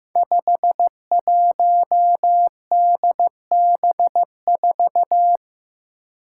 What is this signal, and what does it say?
Morse code, heard as a keyed tone, about 15 wpm, 700 Hz. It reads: 51DB4
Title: Five Challenging Letter-Number Combinations 15wpm